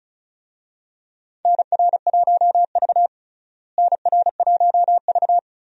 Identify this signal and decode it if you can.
Morse code, heard as a keyed tone, about 34 wpm, 700 Hz. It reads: NR1V NR1V
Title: USA Call Signs - 4 Characters - Repeated 2x - 35wpm